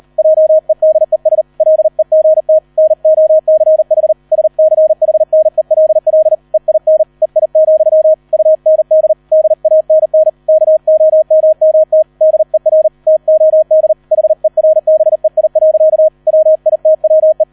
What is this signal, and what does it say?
Morse code, heard as a keyed tone, about 38 wpm, 630 Hz, with radio noise. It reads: JEDES LEGT NOCH SCHNELL EIN EI, UND DANN KOMMT DER TOD HERBEI. WITWE